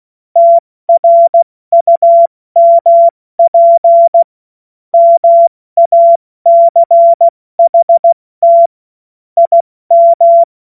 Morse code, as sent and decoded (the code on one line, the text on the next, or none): - .-. ..- -- .--. / -- .- -.-. .... - / .. --
TRUMP MACHT IM